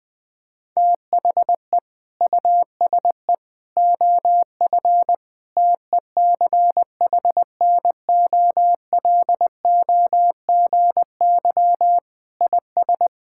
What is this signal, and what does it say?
Morse code, heard as a keyed tone, about 20 wpm, 700 Hz. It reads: THE USE OF TECHNOLOGY IS
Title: Sentences with 2-Word Phrases - 20wpm